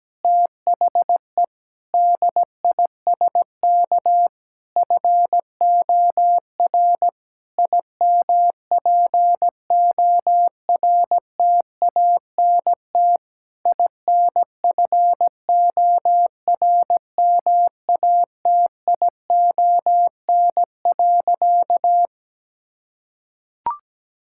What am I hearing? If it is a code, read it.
Morse code, heard as a keyed tone, about 17 wpm, 700 Hz. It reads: THE DISK FOR IMPORTANT INFORMATION.
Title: Sentences from Top 1000 Words - Review Part 6 - 17wpm